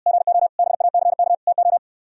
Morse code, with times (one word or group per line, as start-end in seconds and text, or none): 0.1-0.5 GO
0.6-1.3 BACK
1.5-1.8 TO